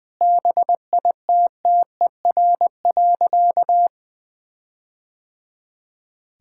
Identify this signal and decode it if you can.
Morse code, heard as a keyed tone, about 20 wpm, 700 Hz. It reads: BITTER.
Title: All-Age Appropriate Jokes - 20wpm